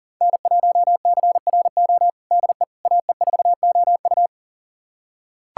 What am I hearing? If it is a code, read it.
Morse code, heard as a keyed tone, about 40 wpm, 700 Hz. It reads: N1CRO DE AE4OU
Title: USA Callsign DE Callsign - 5 Characters - 40wpm